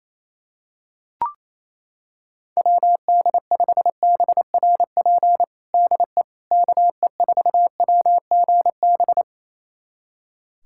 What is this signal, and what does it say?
Morse code, heard as a keyed tone, about 28 wpm, 700 Hz. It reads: WD5BRP DE KE4WGB
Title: USA Callsign DE Callsign 28wpm